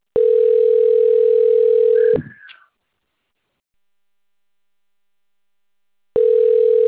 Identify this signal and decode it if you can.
Ringback tone (North America), 440 + 480 Hz.